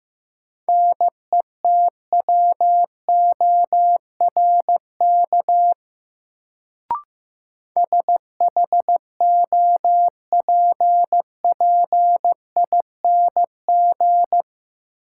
Morse code, heard as a keyed tone, about 15 wpm, 700 Hz. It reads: NETWORK SHOPPING
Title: Sentences from Top 1000 Words - Review Part 7 - New Word Drill - 15wpm